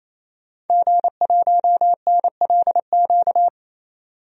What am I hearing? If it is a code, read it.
Morse code, heard as a keyed tone, about 28 wpm, 700 Hz. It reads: G1NLQ
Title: DXCC-100 Call Signs - 5 Characters - 28wpm